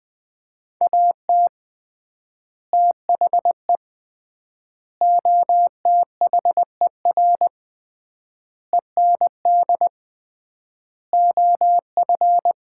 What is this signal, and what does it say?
Morse code, heard as a keyed tone, about 20 wpm, 700 Hz. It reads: AT THE OTHER END OF
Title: 3x Word-Spacing - Sets of 5 Words - 20wpm